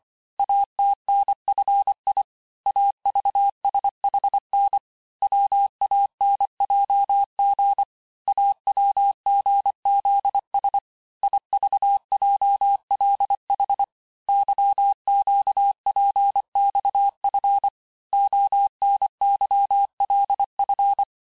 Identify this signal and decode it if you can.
Morse code, heard as a keyed tone, about 25 wpm, 800 Hz. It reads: ATNFI AVSHN WANJG AWGZS IVJLH YQPXF ONYLF
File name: texte-chiffre-vitesse-23mpm